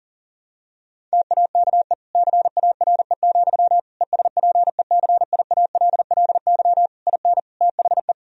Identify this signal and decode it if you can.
Morse code, heard as a keyed tone, about 40 wpm, 700 Hz. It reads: TAKE CARE, ESPECIALLY IN THE
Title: Sentences from Top 1000 Words - Review Part 4 - 40wpm